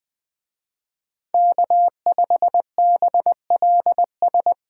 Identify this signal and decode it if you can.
Morse code, heard as a keyed tone, about 21 wpm, 700 Hz. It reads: K5BLS
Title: Speed Racing - USA 1x3 Call Signs 17wpm